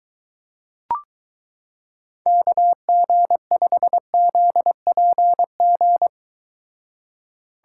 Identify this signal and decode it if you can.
Morse code, heard as a keyed tone, about 23 wpm, 700 Hz. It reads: KG5ZPG